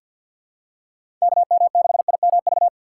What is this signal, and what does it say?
Morse code, heard as a keyed tone, about 50 wpm, 700 Hz. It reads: KM6IMV
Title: USA Call Signs - 6 Characters - 3x Character Spacing - 50wpm